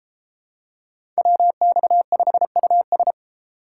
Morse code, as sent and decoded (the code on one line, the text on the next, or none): .-- -..- ..... ..- ...
WX5US